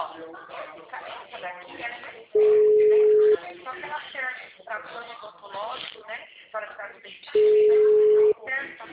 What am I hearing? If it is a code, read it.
Ringback tone (Europe, 425 Hz), with speech.
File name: out- -2001- - - .8696